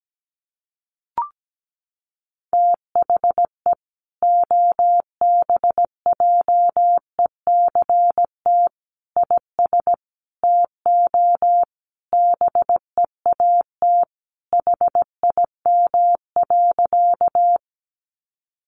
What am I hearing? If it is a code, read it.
Morse code, heard as a keyed tone, about 17 wpm, 700 Hz. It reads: THE OBJECT IS TO BEAT HIM.
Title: Sentences from Top 1300 Words 17wpm